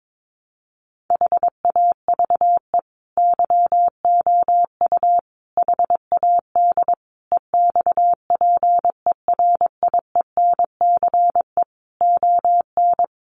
Morse code, heard as a keyed tone, about 22 wpm, 700 Hz. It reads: HAVE YOU HAD EXPERIENCE ON
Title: Sentences from Top 1200 Words 22wpm